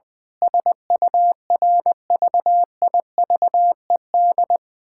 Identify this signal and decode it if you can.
Morse code, heard as a keyed tone, about 21 wpm, 700 Hz. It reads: SURVIVED
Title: Sentences with 7-Word Phrases - 20wpm